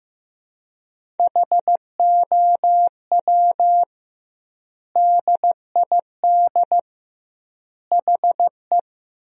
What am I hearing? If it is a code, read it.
Morse code, heard as a keyed tone, about 15 wpm, 700 Hz. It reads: HOW DID HE
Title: 2x Word-Spacing - Sentences from Top 100 Words - 15wpm